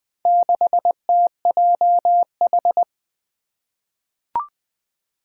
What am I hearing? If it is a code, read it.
Morse code, heard as a keyed tone, about 20 wpm, 700 Hz. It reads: 6TJH
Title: Four Challenging Letter-Number Combinations 20wpm